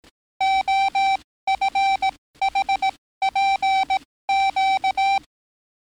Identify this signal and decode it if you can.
Morse code, heard as a keyed tone, about 18 wpm, spaced as for 15 wpm, 780 Hz. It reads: OFHPQ
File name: training / NAVPERS 11049to11060 / NAVPERS 11056 RA RB